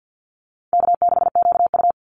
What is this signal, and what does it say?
Morse code, heard as a keyed tone, about 50 wpm, 700 Hz. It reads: K6QU
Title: Easy Call Signs - 50wpm